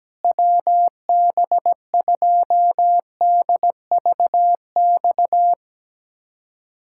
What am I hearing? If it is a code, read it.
Morse code, heard as a keyed tone, about 17 wpm, 700 Hz. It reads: WB2DVX
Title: USA Callsign DE Callsign - 6 Characters - 17wpm